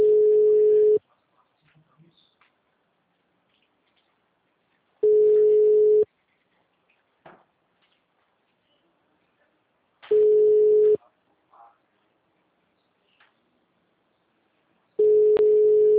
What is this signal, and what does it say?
Ringback tone (Europe, 425 Hz).